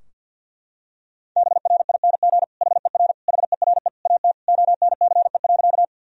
Morse code, heard as a keyed tone, about 47 wpm, 700 Hz. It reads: BRING HER HERE AT ONCE.
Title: Sentences from Top 1000 Words - Review Part 3 - 50wpm